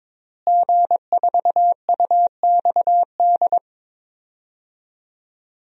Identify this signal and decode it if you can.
Morse code, heard as a keyed tone, about 22 wpm, 700 Hz. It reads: G4UXD
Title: DXCC-100 Call Signs - 5 Characters - Repeated 2x - 22wpm